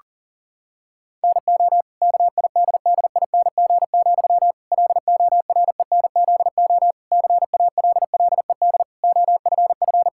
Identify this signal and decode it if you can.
Morse code, heard as a keyed tone, about 37 wpm, 700 Hz. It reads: NO KIDDING, LORENZO CALLED OFF
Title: Pangram Challenge v2 - 40wpm